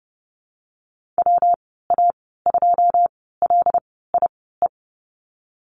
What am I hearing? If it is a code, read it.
Morse code, heard as a keyed tone, about 30 wpm, spaced as for 17 wpm, 700 Hz. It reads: WA2LIE
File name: course / 30:10wpm / 116 Lesson 19g - Letter w - Call Signs 30wpm